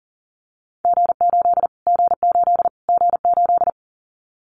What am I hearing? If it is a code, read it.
Morse code, heard as a keyed tone, about 37 wpm, 700 Hz. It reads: G8 G8 G8
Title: Callsign Prefixes for Top 25 DXCC - Repeated 3x - 40wpm